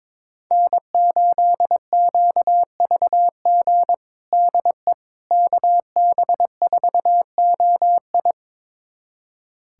Morse code, heard as a keyed tone, about 22 wpm, 700 Hz. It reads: N8QVG DE KB4OI
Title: USA Callsign DE Callsign - 5 Characters - 22wpm